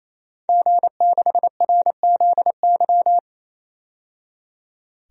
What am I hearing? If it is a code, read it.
Morse code, heard as a keyed tone, about 28 wpm, 700 Hz. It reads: G6RZY
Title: DXCC-100 Call Signs - 5 Characters - 28wpm